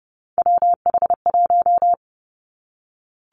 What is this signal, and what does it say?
Morse code, heard as a keyed tone, about 30 wpm, 700 Hz. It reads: WH1
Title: Three Challenging Letter-Number Combinations 30wpm